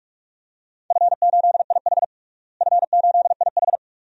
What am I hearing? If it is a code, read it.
Morse code, heard as a keyed tone, about 46 wpm, 700 Hz. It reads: F8IH F8IH
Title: DXCC-100 Call Signs - 4 Characters - Repeated 2x - 45wpm